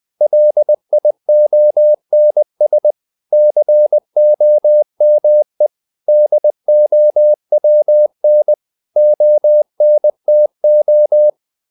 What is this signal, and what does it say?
Morse code, heard as a keyed tone, about 20 wpm, 600 Hz. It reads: LIONS COME DOWN ONTO
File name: the old man and the sea part 2 20wpm